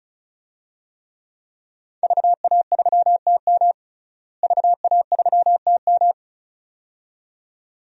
Morse code, heard as a keyed tone, about 34 wpm, 700 Hz. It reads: VA3TM VA3TM
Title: DXCC-100 Call Signs - 5 Characters - Repeated 2x - 35wpm